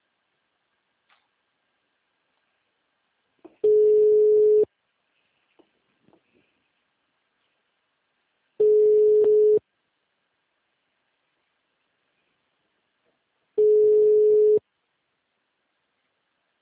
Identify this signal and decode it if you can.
Ringback tone (Europe, 425 Hz).